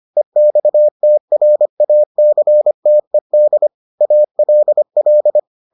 Morse code, heard as a keyed tone, about 25 wpm, 600 Hz. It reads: EXTRACTED ALL